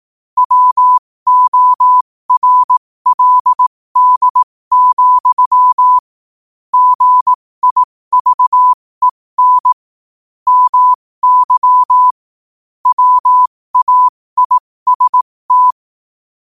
Morse code, heard as a keyed tone, about 18 wpm, spaced as for 15 wpm, 1000 Hz. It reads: WORLD, GIVEN MY WAIST
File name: qotdfiles / 15wpm